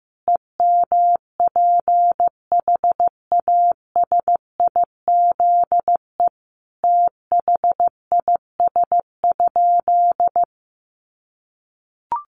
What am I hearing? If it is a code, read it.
Morse code, heard as a keyed tone, about 15 wpm, 700 Hz. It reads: EMPHASIZE THIS?
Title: Sentences from Top 1500 Words 15wpm